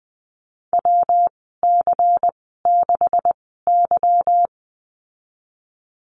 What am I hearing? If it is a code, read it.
Morse code, heard as a keyed tone, about 20 wpm, spaced as for 15 wpm, 700 Hz. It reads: WC6Y
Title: 161 Lesson 26g - Letter g - Call Signs 20wpm